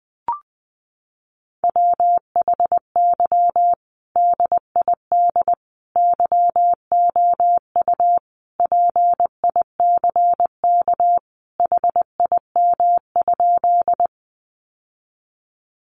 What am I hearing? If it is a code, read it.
Morse code, heard as a keyed tone, about 20 wpm, 700 Hz. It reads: WHY DID YOU PICK HIM?
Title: Sentences from Top 1000 Words 20wpm